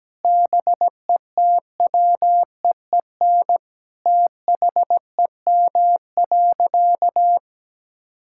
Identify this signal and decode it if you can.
Morse code, heard as a keyed tone, about 17 wpm, 700 Hz. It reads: BETWEEN THEM.
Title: Sentences from Top 1000 Words - Review Part 6 - 17wpm